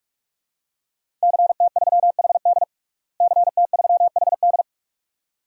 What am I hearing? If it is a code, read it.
Morse code, heard as a keyed tone, about 43 wpm, 700 Hz. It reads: CT3HD CT3HD
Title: DXCC-100 Call Signs - 5 Characters - Repeated 2x - 45wpm